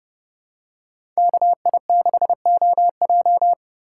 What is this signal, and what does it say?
Morse code, heard as a keyed tone, about 30 wpm, 700 Hz. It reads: KI6OJ
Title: USA Call Signs - 5 Characters - 3x Character Spacing - 30wpm